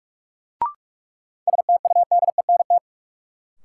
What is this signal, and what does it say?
Morse code, heard as a keyed tone, about 46 wpm, 700 Hz. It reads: STUDENT
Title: Sentences from Top 1000 Words - New Word Drill - 45wpm